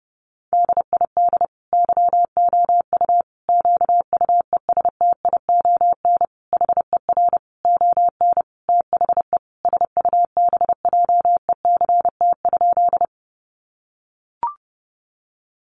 Morse code, heard as a keyed tone, about 30 wpm, 700 Hz. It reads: DID YOU QUESTION HER ON THE SUBJECT?
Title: Sentences from Top 800 Words 30wpm